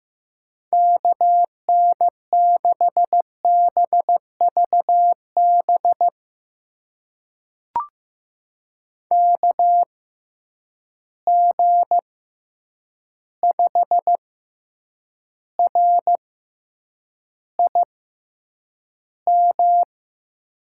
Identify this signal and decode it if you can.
Morse code, heard as a keyed tone, about 15 wpm, spaced as for 8 wpm, 700 Hz. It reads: KN6BVB KG5RIM
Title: USA Call Signs - 6 Characters - 6x Character Spacing - 15wpm